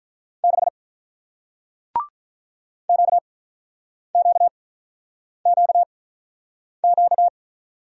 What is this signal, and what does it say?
Morse code, heard as a keyed tone, about 48 wpm, 700 Hz. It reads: 6 Q Q Q Q